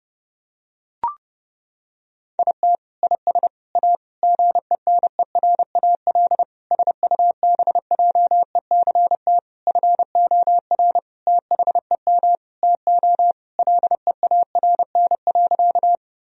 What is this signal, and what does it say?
Morse code, heard as a keyed tone, about 30 wpm, 700 Hz. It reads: IT IS A GENERAL SUBJECT FOR THEM TO LEARN.
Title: Sentences from Top 1000 Words - Review Part 3 - 30wpm